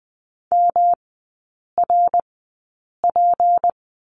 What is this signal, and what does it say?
Morse code, heard as a keyed tone, about 20 wpm, spaced as for 8 wpm, 700 Hz. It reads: MRP
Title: 152 Lesson 25e - Letter p - Three Characters 20wpm